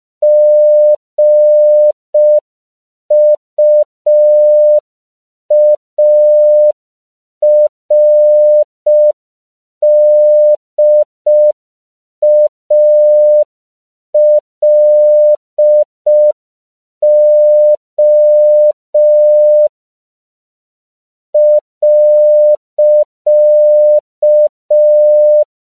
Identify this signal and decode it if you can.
Morse code, heard as a keyed tone, about 5 wpm, 600 Hz. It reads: GUARDALO.
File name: 5PPM